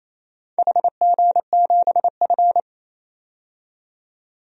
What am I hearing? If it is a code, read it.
Morse code, heard as a keyed tone, about 27 wpm, 690 Hz. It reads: HG7F
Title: DXCC-100 Call Signs - 4 Characters - 28wpm